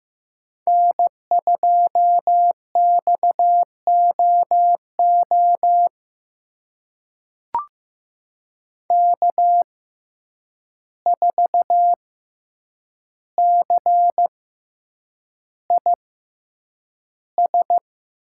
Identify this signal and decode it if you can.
Morse code, heard as a keyed tone, about 15 wpm, spaced as for 8 wpm, 700 Hz. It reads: N2XOO K4CIS
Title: USA Call Signs - 5 Characters - 6x Character Spacing - 15wpm